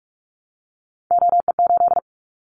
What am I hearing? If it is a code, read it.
Morse code, heard as a keyed tone, about 44 wpm, 700 Hz. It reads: OE8
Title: Callsign Prefixes for Top 100 DXCC - 45wpm